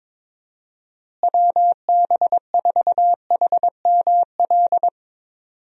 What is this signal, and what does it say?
Morse code, heard as a keyed tone, about 21 wpm, 700 Hz. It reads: WB4HML
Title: USA Call Signs - 6 Characters - Repeated 2x - 22wpm